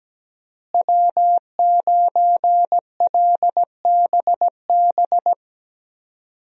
Morse code, heard as a keyed tone, about 17 wpm, 700 Hz. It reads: W9LBB
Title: USA Call Signs - 5 Characters - Repeated 3x - 17wpm